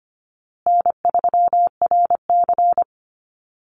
Morse code, heard as a keyed tone, about 25 wpm, 700 Hz. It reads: N3RC
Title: Call Signs Repeated 2x - 25wpm